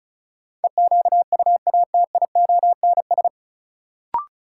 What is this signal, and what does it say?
Morse code, heard as a keyed tone, about 34 wpm, 700 Hz. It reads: EQUATIONS
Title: Sets of 5 Words - New Word Drill - 35wpm